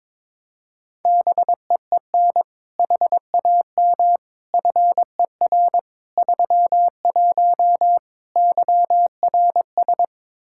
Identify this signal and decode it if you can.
Morse code, heard as a keyed tone, about 22 wpm, 700 Hz. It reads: BEEN HAM FER 31 YRS
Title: K1USN SST QSO Practice 22wpm